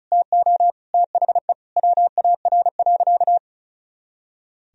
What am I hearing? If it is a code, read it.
Morse code, heard as a keyed tone, about 36 wpm, 700 Hz. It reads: TO THE WAR.
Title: Sentences from Top 400 Words and Abbreviations 35wpm